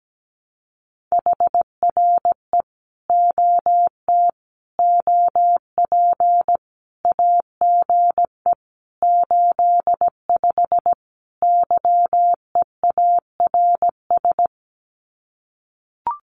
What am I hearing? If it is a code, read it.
Morse code, heard as a keyed tone, about 17 wpm, 700 Hz. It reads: HRE OT OP AGE 85 YEARS
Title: Common QSO Phrases 17wpm